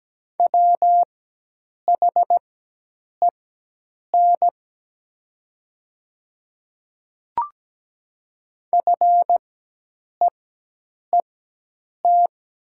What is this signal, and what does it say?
Morse code, heard as a keyed tone, about 17 wpm, spaced as for 8 wpm, 700 Hz. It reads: WHEN FEET
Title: 4x Character Spacing - Top 300 Words - 17wpm